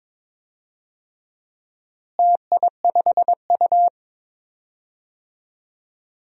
Morse code, heard as a keyed tone, about 22 wpm, 700 Hz. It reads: TI5U
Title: DXCC-100 Call Signs - 4 Characters - Repeated 3x - 22wpm